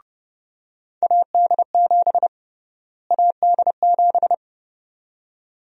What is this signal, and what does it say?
Morse code, heard as a keyed tone, about 32 wpm, 700 Hz. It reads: AD7 AD7
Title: USA Call Sign Prefixes - Repeated 2x - 30wpm